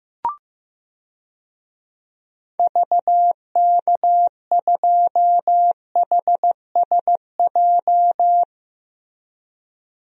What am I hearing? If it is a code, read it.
Morse code, heard as a keyed tone, about 15 wpm, 700 Hz. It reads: VK2HSJ
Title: DXCC-100 Call Signs - 6 Characters - Repeated 3x - 15wpm